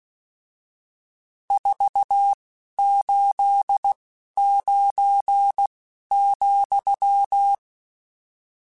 Morse code, heard as a keyed tone, about 16 wpm, 800 Hz. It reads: 4 8 9 ,
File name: RSTs / RST-mp3s / Various RST @10